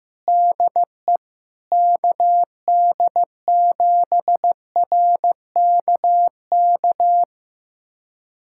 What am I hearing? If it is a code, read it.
Morse code, heard as a keyed tone, about 15 wpm, 700 Hz. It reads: DE KD7RKK
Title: USA Callsign DE Callsign - 6 Characters - 15wpm